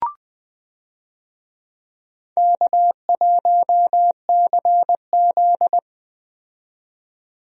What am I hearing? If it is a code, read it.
Morse code, heard as a keyed tone, about 20 wpm, 700 Hz. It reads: K1CZ
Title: USA Call Signs - 4 Characters - 20wpm